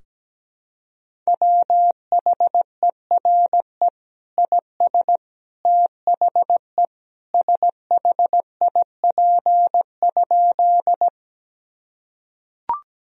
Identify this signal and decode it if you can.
Morse code, heard as a keyed tone, about 17 wpm, 700 Hz. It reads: WHERE IS THE SHIP?
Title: Sentences from Top 1000 Words - Review Part 2 - 17wpm